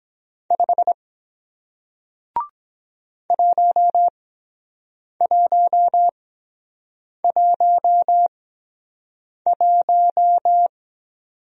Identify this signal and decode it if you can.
Morse code, heard as a keyed tone, about 23 wpm, spaced as for 7 wpm, 700 Hz. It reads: 5 1111